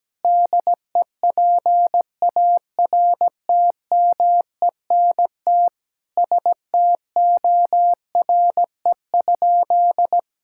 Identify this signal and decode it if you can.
Morse code, heard as a keyed tone, about 17 wpm, 700 Hz. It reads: DEPARTMENT STORE?
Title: Sentences from Top 1000 Words - Review Part 5 - 17wpm